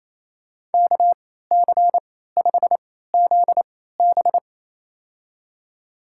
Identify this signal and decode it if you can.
Morse code, heard as a keyed tone, about 28 wpm, spaced as for 16 wpm, 700 Hz. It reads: KC5ZB
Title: USA Call Signs - 3x Character Spacing - 28wpm